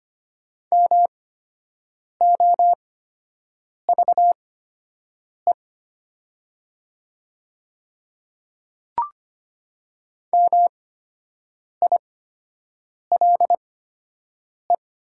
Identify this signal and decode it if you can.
Morse code, heard as a keyed tone, about 25 wpm, spaced as for 7 wpm, 700 Hz. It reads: MOVE MILE
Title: 8x Character Spacing - Top 300 Words - 25wpm